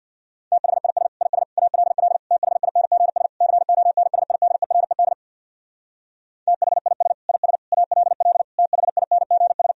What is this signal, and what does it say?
Morse code, heard as a keyed tone, about 64 wpm, 700 Hz. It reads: THIS IS ALL THINGS CONSIDERED THIS IS ALL THINGS